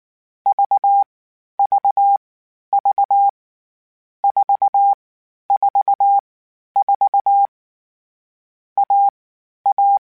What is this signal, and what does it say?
Morse code, heard as a keyed tone, about 19 wpm, 800 Hz. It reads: V V V 4 4 4 A A
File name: Difficult Combos / Difficult Combos-mp3s / AUV AUV4 @18